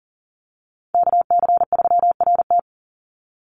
Morse code, heard as a keyed tone, about 41 wpm, 700 Hz. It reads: KC3RT
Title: NA SOTA Chaser Callsigns 40wpm